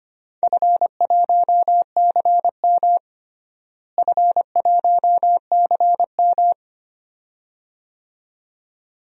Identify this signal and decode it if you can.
Morse code, heard as a keyed tone, about 25 wpm, 700 Hz. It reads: F1CM F1CM
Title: DXCC-100 Call Signs - 4 Characters - Repeated 2x - 25wpm